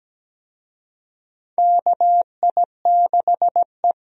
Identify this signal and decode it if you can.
Morse code, heard as a keyed tone, about 17 wpm, 700 Hz. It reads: KI6E
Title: USA Call Signs - 4 Characters - 17wpm